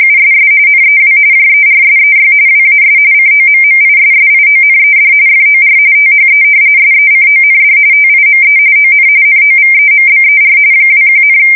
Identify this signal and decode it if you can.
RTTY, about 45.45 baud. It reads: USS NEW JERSEY WAS COMMISSIONED FOR THE SECOND TIME ON NOVEMBER 21,
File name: fsk170 / battleshipnj